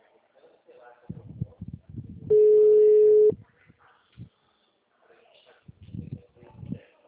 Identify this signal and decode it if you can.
Ringback tone (Europe, 425 Hz), with speech.